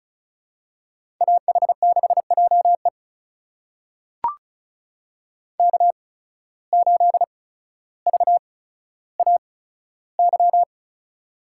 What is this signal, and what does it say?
Morse code, heard as a keyed tone, about 36 wpm, spaced as for 15 wpm, 700 Hz. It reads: AH6JE K8VAY
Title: USA Call Signs - 5 Characters - 8x Character Spacing - 35wpm